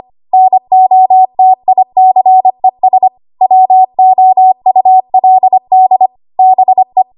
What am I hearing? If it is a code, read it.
Morse code, heard as a keyed tone, about 25 wpm, 750 Hz. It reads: NOTICES WOULD BE
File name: Archive / 25 WPM / 25WPM